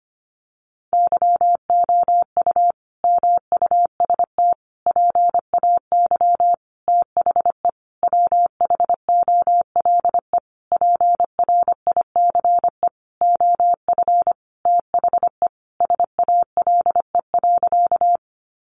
Morse code, heard as a keyed tone, about 25 wpm, 700 Hz. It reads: YOU MUST PAY THE WHOLE PRICE OF THE SALE.
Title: Sentences from Top 1400 Words 25wpm